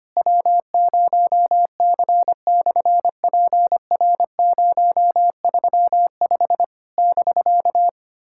Morse code, heard as a keyed tone, about 25 wpm, 700 Hz. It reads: W0C/PR035 <BK>
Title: SOTA QSO Practice 25wpm